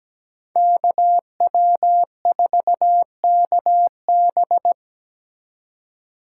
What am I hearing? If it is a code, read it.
Morse code, heard as a keyed tone, about 20 wpm, 700 Hz. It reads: KW4KB
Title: Speed Racing - USA Call Signs 17wpm